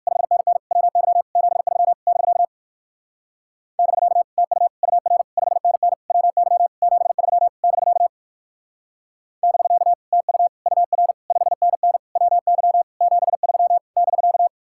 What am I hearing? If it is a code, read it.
Morse code, heard as a keyed tone, about 61 wpm, 700 Hz. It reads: 5NN WY 73 <BK> <BK> TU UR 5NN WY 73 <BK> <BK> TU UR 5NN WY 73 <BK>